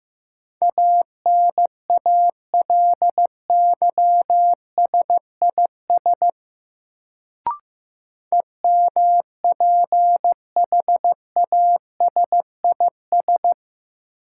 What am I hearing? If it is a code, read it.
Morse code, heard as a keyed tone, about 15 wpm, 700 Hz. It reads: ANALYSIS EMPHASIS
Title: Sentences from Top 1400 Words - New Word Drill Alt - 15wpm